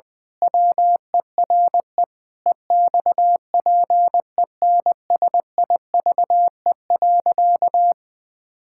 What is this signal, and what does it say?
Morse code, heard as a keyed tone, about 20 wpm, 700 Hz. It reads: WERE EXPENSIVE.
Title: Pangram Challenge v2 - 20wpm